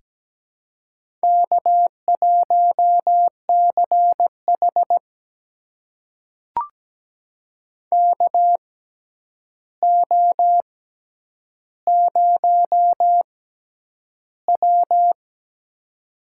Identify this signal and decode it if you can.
Morse code, heard as a keyed tone, about 17 wpm, spaced as for 9 wpm, 700 Hz. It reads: K1CH KO0W
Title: USA Call Signs - 4 Characters - 6x Character Spacing - 17wpm